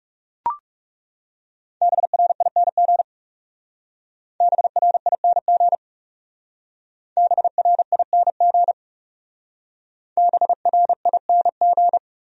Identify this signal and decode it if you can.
Morse code, heard as a keyed tone, about 37 wpm, 700 Hz. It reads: BRING BRING BRING BRING
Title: Speed-Racing Top 500 Words 30wpm